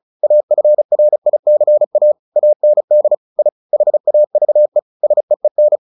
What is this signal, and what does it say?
Morse code, heard as a keyed tone, about 34 wpm, 600 Hz. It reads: AFRICA AND I HAVE SEEN
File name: the old man and the sea / the old man and the sea 1 35wpm